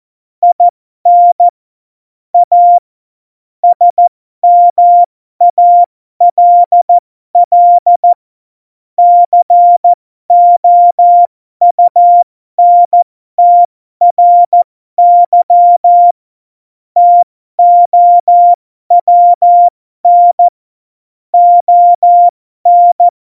Morse code, heard as a keyed tone, about 14 wpm, spaced as for 12 wpm, 700 Hz. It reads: IN A SMALL COUNTRY TOWN ON
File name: Morsekurs / Witze